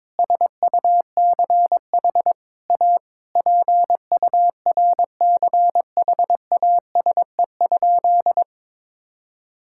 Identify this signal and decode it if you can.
Morse code, heard as a keyed tone, about 21 wpm, 700 Hz. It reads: SUCH A PURCHASE?
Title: Sentences from Top 1000 Words - Review Part 6 - 22wpm